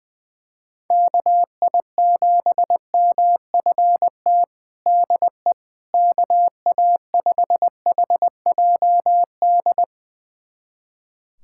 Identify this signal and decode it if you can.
Morse code, heard as a keyed tone, about 21 wpm, 700 Hz. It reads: KI7MFT DE KA5HJD
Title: USA Callsign DE Callsign - 6 Characters - 20wpm